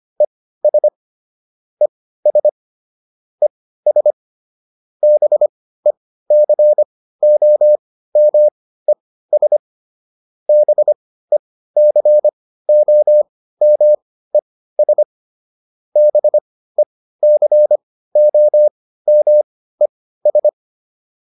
Morse code, heard as a keyed tone, about 25 wpm, spaced as for 15 wpm, 600 Hz. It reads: ES ES ES BECOMES BECOMES BECOMES